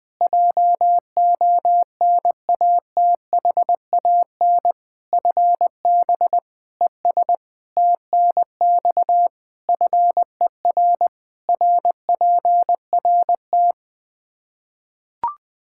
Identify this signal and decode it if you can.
Morse code, heard as a keyed tone, about 21 wpm, 700 Hz. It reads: JONATHAN FB ES TNX FER RPRT
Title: Practice QSO - Exchange 2 - Info - 20wpm